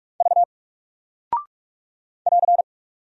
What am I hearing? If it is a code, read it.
Morse code, heard as a keyed tone, about 45 wpm, 700 Hz. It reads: V <AR>
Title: Single Character (Letter:Number:Punctuation:Prosign) 45wpm